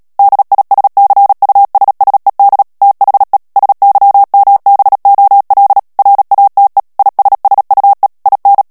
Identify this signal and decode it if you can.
Morse code, heard as a keyed tone, about 37 wpm, 800 Hz. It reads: DISCUSSED THE SYMBOL RATE ISSUE IN
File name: morse mp3 / ANGLAIS texte clair 3 vitesse 35 mots minute